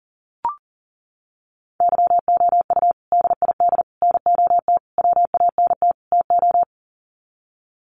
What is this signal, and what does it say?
Morse code, heard as a keyed tone, about 40 wpm, 700 Hz. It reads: YOU DID NOT WANT TO
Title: Sets of 5 Words Encore2 40wpm